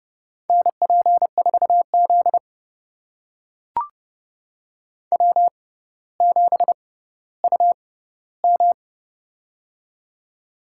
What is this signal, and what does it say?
Morse code, heard as a keyed tone, about 29 wpm, spaced as for 15 wpm, 700 Hz. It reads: NP4Z W7UM
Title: USA Call Signs - 4 Characters - 6x Character Spacing - 30wpm